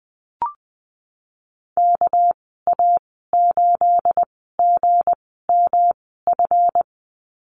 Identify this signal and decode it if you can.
Morse code, heard as a keyed tone, about 20 wpm, 700 Hz. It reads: K A 8 G M F
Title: 238 Lesson 38g - Letter q - Call Signs 20wpm